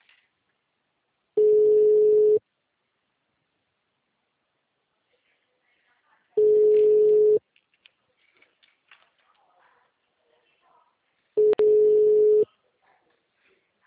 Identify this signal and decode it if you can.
Ringback tone (Europe, 425 Hz), with speech.